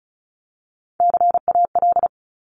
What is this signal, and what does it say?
Morse code, heard as a keyed tone, about 36 wpm, 700 Hz. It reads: CAL
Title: Top Three Letter Combinations in Words 35wpm